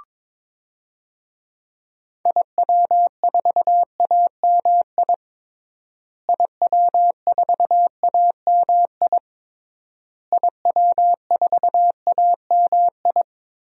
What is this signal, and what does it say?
Morse code, heard as a keyed tone, about 22 wpm, 700 Hz. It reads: IW4AMI IW4AMI IW4AMI
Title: DXCC-100 Call Signs - 6 Characters - Repeated 3x - 22wpm